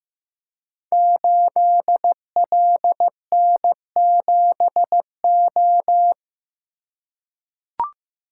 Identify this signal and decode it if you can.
Morse code, heard as a keyed tone, about 15 wpm, 700 Hz. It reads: 8LN7O
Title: Code Groups - 5 Letter:Numbers - 15wpm